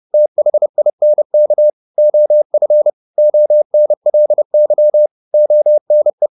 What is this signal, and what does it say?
Morse code, heard as a keyed tone, about 30 wpm, 600 Hz. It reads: THINK OF ONLY ONE
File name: the old man and the sea / the old man and the sea 1 30wpm